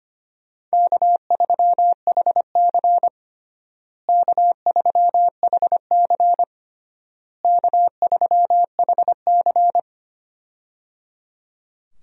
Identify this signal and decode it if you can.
Morse code, heard as a keyed tone, about 25 wpm, 700 Hz. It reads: K3HC K3HC K3HC
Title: USA Call Signs - 4 Characters - Repeated 3x - 25wpm